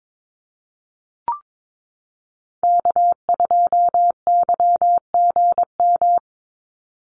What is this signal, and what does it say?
Morse code, heard as a keyed tone, about 22 wpm, 700 Hz. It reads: K2YGM
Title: Call Signs Repeated 2x - 22wpm